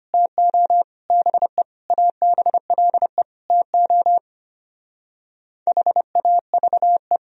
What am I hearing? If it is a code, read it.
Morse code, heard as a keyed tone, about 33 wpm, 700 Hz. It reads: TO BE ABLE TO HAVE